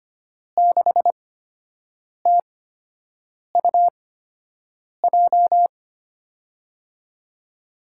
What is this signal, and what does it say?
Morse code, heard as a keyed tone, about 25 wpm, spaced as for 7 wpm, 700 Hz. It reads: 6TUJ
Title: USA Call Signs - 6 Characters - 8x Character Spacing - 25wpm